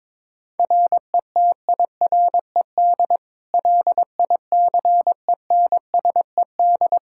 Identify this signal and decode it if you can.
Morse code, heard as a keyed tone, about 22 wpm, 700 Hz. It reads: RETIRED LICENSED
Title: Practice QSO - Exchange 3 - More Info - 22wpm